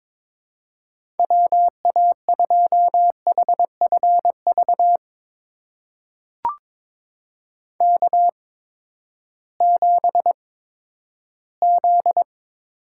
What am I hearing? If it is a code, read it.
Morse code, heard as a keyed tone, about 22 wpm, spaced as for 12 wpm, 700 Hz. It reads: WA2HFV K7Z
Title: USA Call Signs - 8x Character Spacing - 22wpm